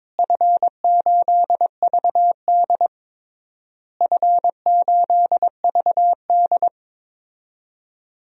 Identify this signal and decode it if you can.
Morse code, heard as a keyed tone, about 22 wpm, 700 Hz. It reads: F8VD F8VD
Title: DXCC-100 Call Signs - 4 Characters - Repeated 3x - 22wpm